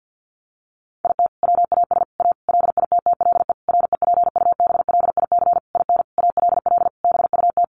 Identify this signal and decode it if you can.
Morse code, heard as a keyed tone, about 50 wpm, 700 Hz. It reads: IT WAS A LITTLE REPUBLIC IN ALL BUT
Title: Sentences from Top 1500 Words 50wpm